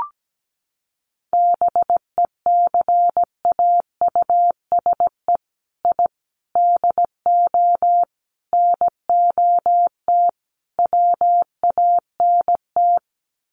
Morse code, heard as a keyed tone, about 17 wpm, 700 Hz. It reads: BECAUSE I DO NOT WANT
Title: Sets of 5 Words Encore 17wpm